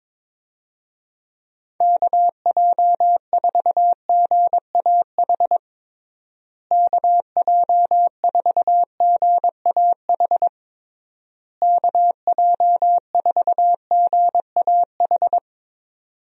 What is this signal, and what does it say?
Morse code, heard as a keyed tone, about 21 wpm, 700 Hz. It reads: KJ4GAH KJ4GAH KJ4GAH
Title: USA Call Signs - 6 Characters - Repeated 3x - 22wpm